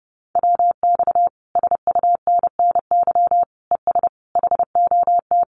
Morse code, heard as a keyed tone, about 30 wpm, 700 Hz. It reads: WX SUNNY ES HOT
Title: Common QSO Phrases Encore 30wpm